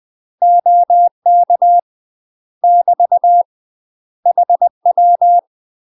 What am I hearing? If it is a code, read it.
Morse code, heard as a keyed tone, about 20 wpm, 700 Hz. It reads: OK <BT> HW
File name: UK MASTER QSOs TRAINING / 20WPM / UK MASTER QSOs 0047